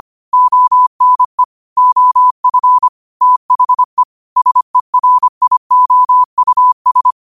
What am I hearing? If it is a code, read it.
Morse code, heard as a keyed tone, about 25 wpm, 1000 Hz. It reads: ONE OF THE SERIOUS
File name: qotdfiles / 25wpm